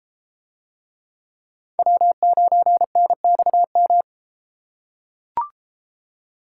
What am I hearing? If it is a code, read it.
Morse code, heard as a keyed tone, about 31 wpm, 700 Hz. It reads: W9NXM